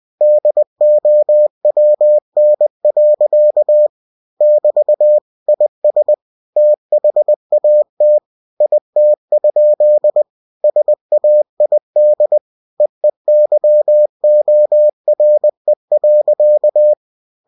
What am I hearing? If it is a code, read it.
Morse code, heard as a keyed tone, about 20 wpm, 600 Hz. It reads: DOWN. <BT> IS THAT IT? SAID EEYORE.